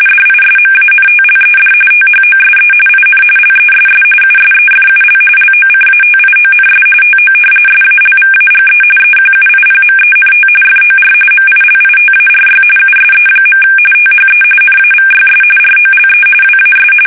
RTTY, about 45.45 baud. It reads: DISMANTLED THOSE IN LESS USEFUL LOCATIONS TO SALVAGE EQUIPMENT FOR MILITARY USE. THE NAVYS ROLE WAS
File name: fsk850